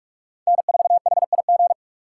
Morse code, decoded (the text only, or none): N4HIG